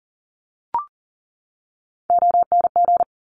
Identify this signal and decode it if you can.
Morse code, heard as a keyed tone, about 40 wpm, 690 Hz. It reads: ONG
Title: Top Three Letter Combinations in Words 40wpm